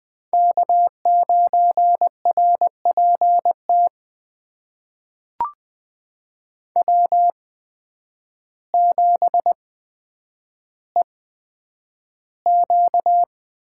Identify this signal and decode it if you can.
Morse code, heard as a keyed tone, about 20 wpm, spaced as for 9 wpm, 700 Hz. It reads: K9RPT W7EQ
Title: USA Call Signs - 8x Character Spacing - 20wpm